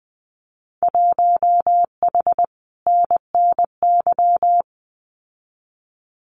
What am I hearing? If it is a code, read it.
Morse code, heard as a keyed tone, about 20 wpm, 700 Hz. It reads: 1H NNY
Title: Winter Field Day Exchanges 20wpm